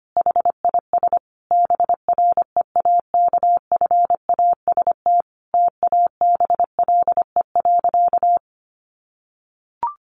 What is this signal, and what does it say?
Morse code, heard as a keyed tone, about 25 wpm, 700 Hz. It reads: HIS BREAKFAST TABLE.
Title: Sentences from Top 1500 Words 25wpm